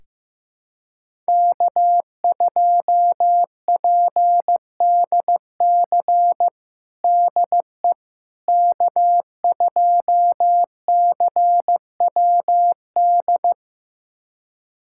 Morse code, heard as a keyed tone, about 15 wpm, 700 Hz. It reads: K2PDC DE K2CWD
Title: USA Callsign DE Callsign - 5 Characters - 15wpm